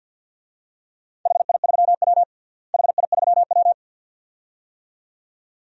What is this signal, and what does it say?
Morse code, heard as a keyed tone, about 51 wpm, 700 Hz. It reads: HI3W HI3W
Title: DXCC-100 Call Signs - 4 Characters - Repeated 2x - 50wpm